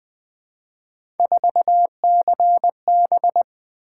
Morse code, heard as a keyed tone, about 20 wpm, 700 Hz. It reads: 4CB
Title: Three Letter-Number Combinations - 20wpm